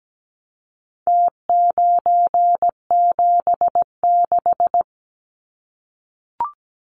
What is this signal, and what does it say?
Morse code, heard as a keyed tone, about 17 wpm, 700 Hz. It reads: T976
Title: Numbers - Contest Style - 4-Digit 17wpm